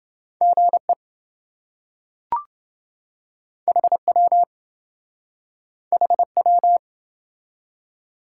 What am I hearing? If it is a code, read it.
Morse code, heard as a keyed tone, about 28 wpm, 700 Hz. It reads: GE HW HW